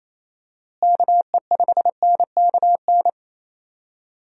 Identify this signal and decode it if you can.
Morse code, heard as a keyed tone, about 27 wpm, 700 Hz. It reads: KE5NKN
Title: USA Call Signs - 6 Characters - Repeated 3x - 28wpm